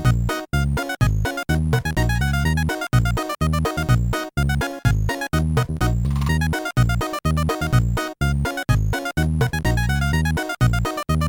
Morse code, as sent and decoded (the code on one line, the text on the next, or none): none